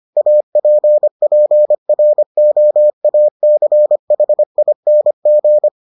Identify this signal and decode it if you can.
Morse code, heard as a keyed tone, about 25 wpm, 600 Hz. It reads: APPROACHING